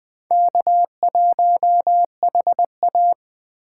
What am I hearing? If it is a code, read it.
Morse code, heard as a keyed tone, about 21 wpm, 700 Hz. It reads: K1HA
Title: USA Call Signs - 4 Characters - 3x Character Spacing - 20wpm